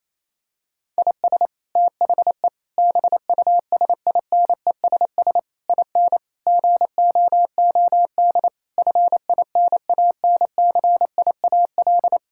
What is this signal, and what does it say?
Morse code, heard as a keyed tone, about 27 wpm, 700 Hz. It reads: IS THE BUSINESS IN GOOD FINANCIAL
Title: Sentences from Top 1000 Words - Review Part 5 - 28wpm